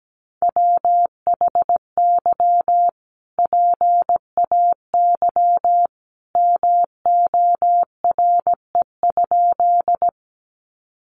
Morse code, heard as a keyed tone, about 17 wpm, 700 Hz. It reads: WHY PAY MORE?
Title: Sentences from Top 700 Words 17wpm